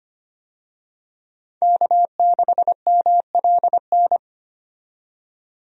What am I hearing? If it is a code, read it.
Morse code, heard as a keyed tone, about 25 wpm, 700 Hz. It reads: K6MLN
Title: USA Call Signs - 5 Characters - 25wpm